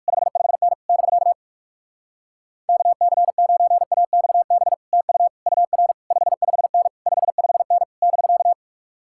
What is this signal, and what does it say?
Morse code, heard as a keyed tone, about 64 wpm, spaced as for 54 wpm, 700 Hz. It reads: 55N <BK> KC9AXB TU UR 55N 55N <BK>
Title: Speed-Racing Common QSO Phrases Encore - 45wpm